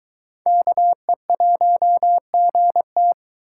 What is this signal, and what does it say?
Morse code, heard as a keyed tone, about 21 wpm, 700 Hz. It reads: KE1GT